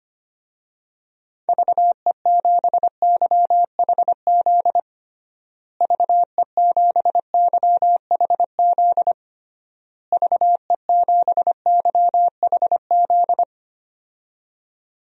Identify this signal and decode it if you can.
Morse code, heard as a keyed tone, about 25 wpm, 700 Hz. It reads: VE7YHZ VE7YHZ VE7YHZ
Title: DXCC-100 Call Signs - 6 Characters - Repeated 3x - 25wpm